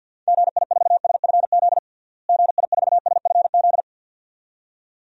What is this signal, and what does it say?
Morse code, heard as a keyed tone, about 53 wpm, 700 Hz. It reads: GI4SFZ GI4SFZ
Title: DXCC-100 Call Signs - 6 Characters - Repeated 2x - 50wpm